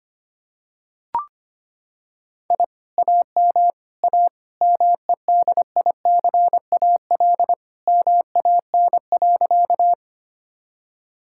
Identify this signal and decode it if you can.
Morse code, heard as a keyed tone, about 25 wpm, 700 Hz. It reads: I AM A MEDICAL MAN.
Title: Sentences from Top 1000 Words - Review Part 5 - 25wpm